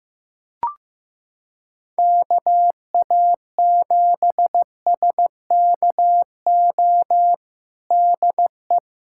Morse code, heard as a keyed tone, about 15 wpm, 700 Hz. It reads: KA7SKO DE
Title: USA Callsign DE Callsign - 6 Characters - 15wpm